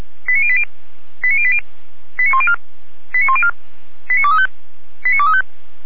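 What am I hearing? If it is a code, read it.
Five-tone selective call (ZVEI), heard as quick runs of five tones, tones of about 70 ms. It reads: 89080 89080 89104 89104 89235 89235
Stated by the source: ZVEI five-tone selective call 89115 received by SDR.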